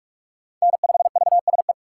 Morse code, heard as a keyed tone, about 42 wpm, 700 Hz. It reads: N5VSE
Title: USA Call Signs - 5 Characters - 2x Character Spacing - 45wpm